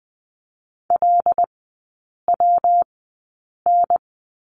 Morse code, heard as a keyed tone, about 20 wpm, spaced as for 8 wpm, 700 Hz. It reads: LWN
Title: 138 Lesson 23e - Letter f - Three Characters 20wpm